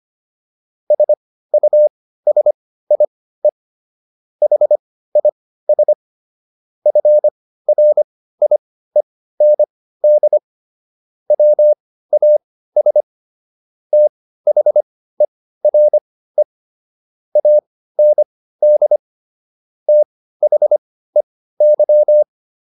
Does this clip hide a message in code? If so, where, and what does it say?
Yes: Morse code from 0.9 to 22.2 s, heard as a keyed tone, about 25 wpm, spaced as for 15 wpm, 600 Hz. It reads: SUSIE HIS FRIEND WAS THERE AND THEY